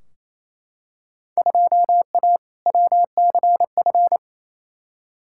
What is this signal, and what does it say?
Morse code, heard as a keyed tone, about 28 wpm, 700 Hz. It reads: 2A WCF
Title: Field Day Exchanges Encore - 28wpm